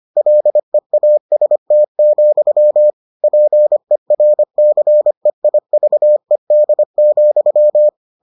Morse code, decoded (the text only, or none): LEAST, PERCEIVED,